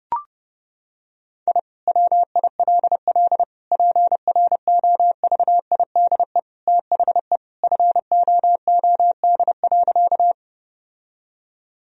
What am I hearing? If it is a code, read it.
Morse code, heard as a keyed tone, about 32 wpm, 700 Hz. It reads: I WILL PROVIDE THE FOOD.
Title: Sentences from Top 1000 Words - Review Part 3 - 30wpm